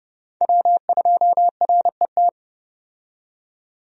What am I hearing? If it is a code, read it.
Morse code, heard as a keyed tone, about 30 wpm, 700 Hz. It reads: W2RET
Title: USA Call Signs - 5 Characters - 30wpm